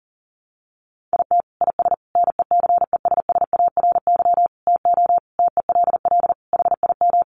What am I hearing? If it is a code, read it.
Morse code, heard as a keyed tone, about 40 wpm, 690 Hz. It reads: IT IS NECESSARY TO TELL HIM
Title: Sentences from Top 800 Words 40wpm